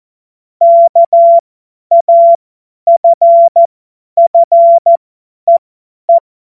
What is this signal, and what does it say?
Morse code, heard as a keyed tone, about 14 wpm, spaced as for 10 wpm, 680 Hz. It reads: KAFFEE
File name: Morsekurs / Witze / Witz 23. Dezember 2025 50 F70